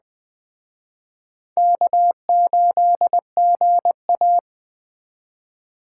Morse code, heard as a keyed tone, about 20 wpm, 700 Hz. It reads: K8GA